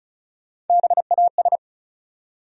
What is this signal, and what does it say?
Morse code, heard as a keyed tone, about 36 wpm, 700 Hz. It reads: BAS
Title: USA Call Sign Postfixes - 3 Letters 35wpm